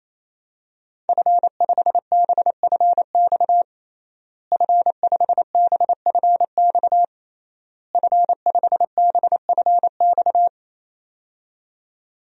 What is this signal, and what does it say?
Morse code, heard as a keyed tone, about 27 wpm, 700 Hz. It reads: F5BFX F5BFX F5BFX
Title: DXCC-100 Call Signs - 5 Characters - Repeated 3x - 28wpm